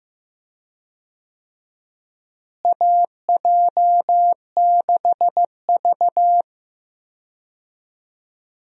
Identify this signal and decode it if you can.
Morse code, heard as a keyed tone, about 15 wpm, 700 Hz. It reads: AJ6V
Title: USA Call Signs - 4 Characters - Repeated 3x - 15wpm